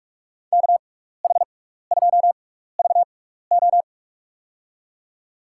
Morse code, heard as a keyed tone, about 47 wpm, spaced as for 16 wpm, 700 Hz. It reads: KH2VO
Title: USA Call Signs - 6x Character Spacing - 45wpm